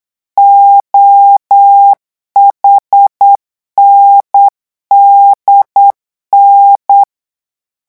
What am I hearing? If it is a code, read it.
Morse code, heard as a keyed tone, about 8 wpm, 800 Hz. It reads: OHNDN
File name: morse mp3 / revision8-vitesse 8 mots